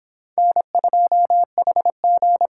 Morse code, heard as a keyed tone, about 26 wpm, 700 Hz. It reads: N2HG